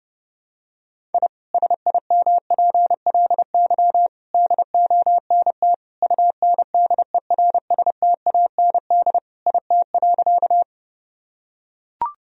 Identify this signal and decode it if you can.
Morse code, heard as a keyed tone, about 30 wpm, 700 Hz. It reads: I SIMPLY DONT UNDERSTAND IT.
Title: Sentences from Top 1000 Words - Review Part 4 - 30wpm